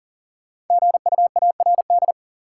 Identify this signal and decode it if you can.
Morse code, heard as a keyed tone, about 37 wpm, 700 Hz. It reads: GUARD
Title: Sentences from Top 700 Words - New Word Drill - 40wpm